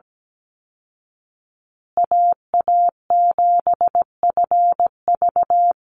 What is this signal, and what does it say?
Morse code, heard as a keyed tone, about 17 wpm, 700 Hz. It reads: AA7FV
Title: Call Signs Repeated 3x - 17wpm